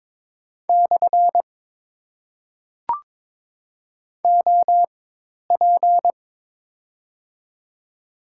Morse code, heard as a keyed tone, about 22 wpm, spaced as for 10 wpm, 690 Hz. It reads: / OP
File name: Top 90 QSO Elements - 4x Character Spacing - 22wpm